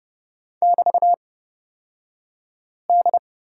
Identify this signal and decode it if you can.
Morse code, heard as a keyed tone, about 30 wpm, spaced as for 5 wpm, 700 Hz. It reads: <BT>D
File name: Lesson 36d - BT - Two Characters 30wpm